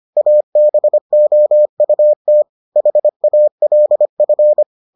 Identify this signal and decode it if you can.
Morse code, heard as a keyed tone, about 25 wpm, 600 Hz. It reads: ABOUT HALF